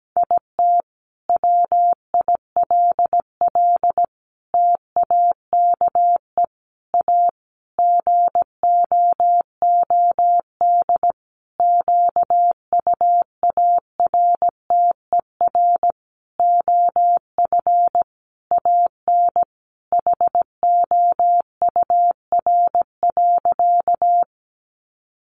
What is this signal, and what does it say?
Morse code, heard as a keyed tone, about 17 wpm, 700 Hz. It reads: IT WILL TAKE A GOOD QUARTER OF AN HOUR.
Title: Sentences from Top 1300 Words 17wpm